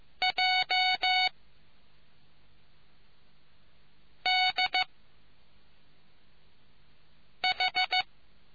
Morse code, heard as a keyed tone, about 14 wpm, 740 Hz. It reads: J D H